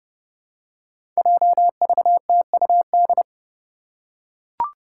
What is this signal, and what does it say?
Morse code, heard as a keyed tone, about 32 wpm, 700 Hz. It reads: JVTUD
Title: Five Challenging Letter-Number Combinations 30wpm